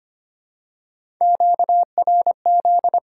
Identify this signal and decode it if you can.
Morse code, heard as a keyed tone, about 25 wpm, 700 Hz. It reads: QRZ
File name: Top 90 QSO Elements - 2x Character Spacing - 25wpm